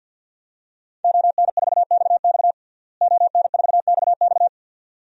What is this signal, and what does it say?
Morse code, heard as a keyed tone, about 52 wpm, 700 Hz. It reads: ON4XX ON4XX
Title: DXCC-100 Call Signs - 5 Characters - Repeated 2x - 50wpm